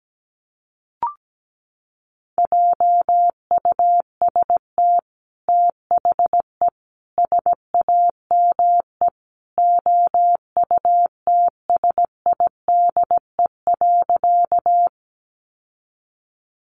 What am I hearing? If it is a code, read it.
Morse code, heard as a keyed tone, about 17 wpm, 700 Hz. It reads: JUST THE SAME OUTSIDE.
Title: Sentences from Top 900 Words 17wpm